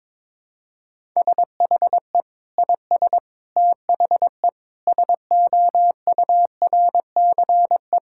Morse code, heard as a keyed tone, about 21 wpm, 700 Hz. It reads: SHE IS THE SOURCE
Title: Sentences from Top 1000 Words - Review Part 5 - 22wpm